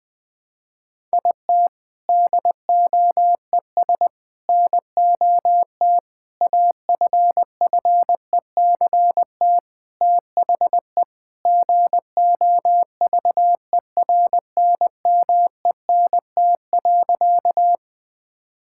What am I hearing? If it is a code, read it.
Morse code, heard as a keyed tone, about 20 wpm, 700 Hz. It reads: IT DOES NOT AFFECT THE GOVERNMENT.
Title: Sentences from Top 1000 Words - Review Part 6 - 20wpm